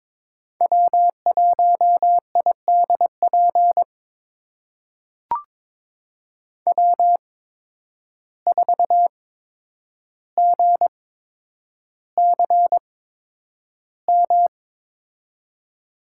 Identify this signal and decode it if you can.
Morse code, heard as a keyed tone, about 22 wpm, spaced as for 9 wpm, 700 Hz. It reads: W1IDP W4GCM
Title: USA Call Signs - 5 Characters - 8x Character Spacing - 22wpm